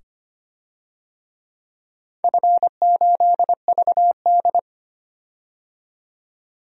Morse code, heard as a keyed tone, about 25 wpm, 700 Hz. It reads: F8VD
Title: DXCC-100 Call Signs - 4 Characters - 25wpm